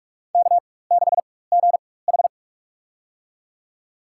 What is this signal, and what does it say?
Morse code, heard as a keyed tone, about 46 wpm, spaced as for 21 wpm, 700 Hz. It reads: K6GH
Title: USA Call Signs - 4 Characters - 4x Character Spacing - 45wpm